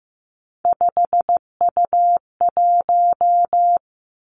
Morse code, heard as a keyed tone, about 15 wpm, 700 Hz. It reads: 5U1
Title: Three Challenging Letter-Number Combinations 15wpm